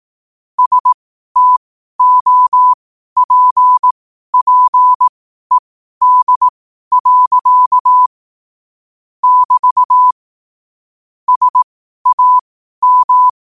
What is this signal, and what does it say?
Morse code, heard as a keyed tone, about 18 wpm, spaced as for 12 wpm, 1000 Hz. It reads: STOPPED. <BT> SAM
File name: qotdfiles / 12wpm